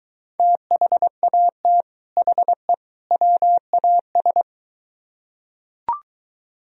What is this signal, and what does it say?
Morse code, heard as a keyed tone, about 23 wpm, 700 Hz. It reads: THAT HE WAS